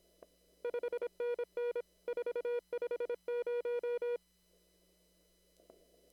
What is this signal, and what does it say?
Morse code, heard as a keyed tone, about 26 wpm, 500 Hz. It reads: 5NN 450
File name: CW / kt5j wpx cw-op-k5pi